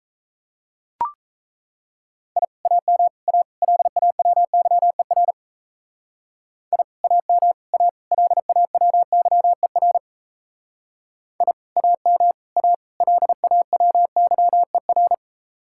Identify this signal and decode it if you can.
Morse code, heard as a keyed tone, about 33 wpm, 700 Hz. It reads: I AM A LAWYER I AM A LAWYER I AM A LAWYER
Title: Speed-Racing Common QSO Phrases Encore - 28wpm